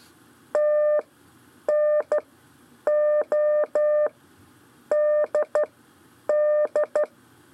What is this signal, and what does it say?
Morse code, heard as a keyed tone, about 12 wpm, spaced as for 8 wpm, 600 Hz. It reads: TNODD
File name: 4 LEZIONE CW 1